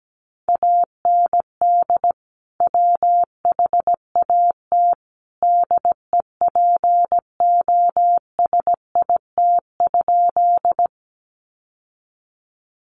Morse code, heard as a keyed tone, about 17 wpm, 700 Hz. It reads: AND WHAT DEPOSIT?
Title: Sentences from Top 1300 Words 17wpm